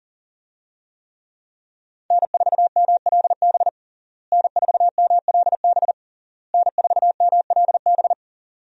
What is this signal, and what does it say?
Morse code, heard as a keyed tone, about 40 wpm, 700 Hz. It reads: N4MLB N4MLB N4MLB
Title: USA Call Signs - 5 Characters - Repeated 3x - 40wpm